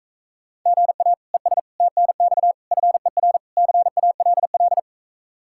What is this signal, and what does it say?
Morse code, heard as a keyed tone, about 45 wpm, 700 Hz. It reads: GA ES TNX FER CALL